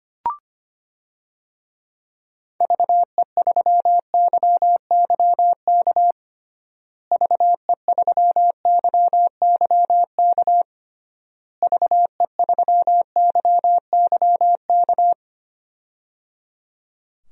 Morse code, heard as a keyed tone, about 25 wpm, 700 Hz. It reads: VE3YYK VE3YYK VE3YYK
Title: DXCC-100 Call Signs - 6 Characters - Repeated 3x - 25wpm